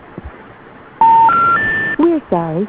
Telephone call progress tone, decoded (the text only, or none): special information tone